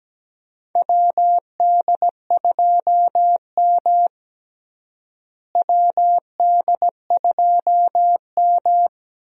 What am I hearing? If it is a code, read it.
Morse code, heard as a keyed tone, about 17 wpm, 700 Hz. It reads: WD2M WD2M
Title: USA Call Signs - 4 Characters - Repeated 3x - 17wpm